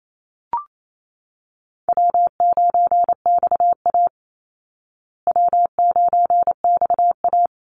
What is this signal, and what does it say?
Morse code, heard as a keyed tone, about 28 wpm, 700 Hz. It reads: W9XA W9XA
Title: Call Signs Repeated 3x - 28wpm